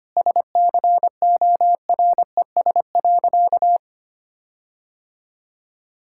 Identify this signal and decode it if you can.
Morse code, heard as a keyed tone, about 25 wpm, 700 Hz. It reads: SCORES.
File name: practice / 25wpm / All-Age Appropriate Jokes - 25wpm